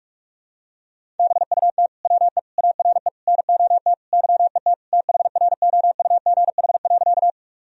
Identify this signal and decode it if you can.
Morse code, heard as a keyed tone, about 41 wpm, 700 Hz. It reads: BUT WE ARE NOT YET THROUGH.
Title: Sentences from Top 1000 Words - Review Part 1 - 45wpm